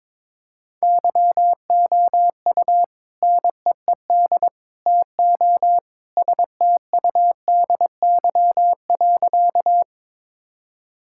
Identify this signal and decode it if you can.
Morse code, heard as a keyed tone, about 22 wpm, 700 Hz. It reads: YOU NEED TO STUDY.
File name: practice / 22wpm / Sentences with 3-Word Phrases - 22wpm